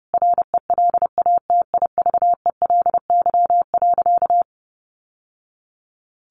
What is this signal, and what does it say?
Morse code, heard as a keyed tone, about 30 wpm, 700 Hz. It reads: RELATIVELY.
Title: Sentences from Top 1100 Words 30wpm